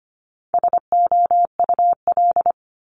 Morse code, heard as a keyed tone, about 25 wpm, 700 Hz. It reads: SOUL
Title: Multi-letter Phonemes Repeated 3x 25wpm